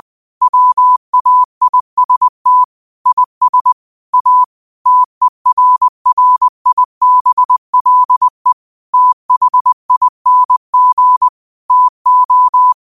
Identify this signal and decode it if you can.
Morse code, heard as a keyed tone, about 20 wpm, 1000 Hz. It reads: WAIST IS A TERRIBLE THING TO
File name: qotdfiles / 20wpm